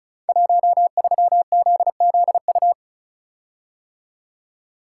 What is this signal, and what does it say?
Morse code, heard as a keyed tone, about 34 wpm, 700 Hz. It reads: 13ZZU
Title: Code Groups - 5 Letter:Numbers - 35wpm